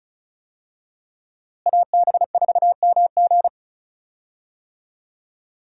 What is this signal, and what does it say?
Morse code, heard as a keyed tone, about 36 wpm, 700 Hz. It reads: AB4MG
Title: USA Call Signs - 5 Characters - Repeated 2x - 35wpm